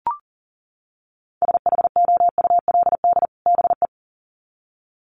Morse code, heard as a keyed tone, about 40 wpm, 700 Hz. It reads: SHOULD BE
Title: Sets of 2 Words 40wpm